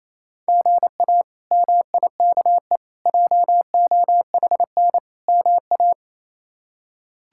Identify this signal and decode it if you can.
Morse code, heard as a keyed tone, about 28 wpm, 700 Hz. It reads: GA MIKE JOHN MA
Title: K1USN SST QSO Practice - 28wpm